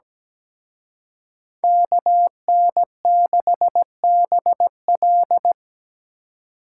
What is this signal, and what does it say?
Morse code, heard as a keyed tone, about 20 wpm, 700 Hz. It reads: KN6BL
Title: Speed Racing - USA Call Signs 17wpm